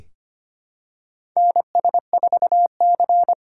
Morse code, heard as a keyed tone, about 25 wpm, 700 Hz. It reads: NS4C
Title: USA Call Signs - 3x Character Spacing - 25wpm